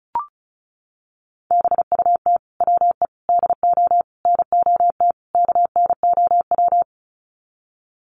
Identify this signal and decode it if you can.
Morse code, heard as a keyed tone, about 34 wpm, 700 Hz. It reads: BUT WE DO NOT KNOW
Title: Sets of 5 Words Encore2 35wpm